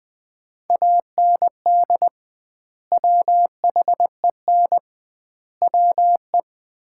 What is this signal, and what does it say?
Morse code, heard as a keyed tone, about 20 wpm, 700 Hz. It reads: AND WHEN WE
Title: 2x Word-Spacing - Sets of 5 Words - 20wpm